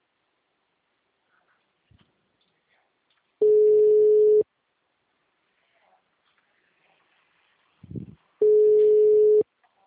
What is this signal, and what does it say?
Ringback tone (Europe, 425 Hz).